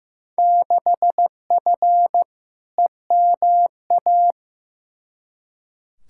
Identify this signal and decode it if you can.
Morse code, heard as a keyed tone, about 15 wpm, 700 Hz. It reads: 6F EMA
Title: Field Day Exchanges Encore - 15wpm